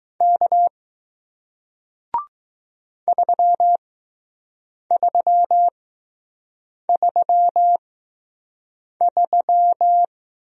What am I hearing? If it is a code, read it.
Morse code, heard as a keyed tone, about 20 wpm, spaced as for 6 wpm, 700 Hz. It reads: K 3333